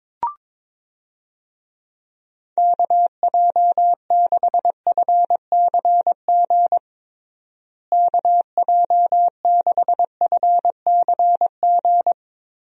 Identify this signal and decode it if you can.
Morse code, heard as a keyed tone, about 22 wpm, 700 Hz. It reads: KJ6FCG KJ6FCG
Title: USA Call Signs - 6 Characters - Repeated 3x - 22wpm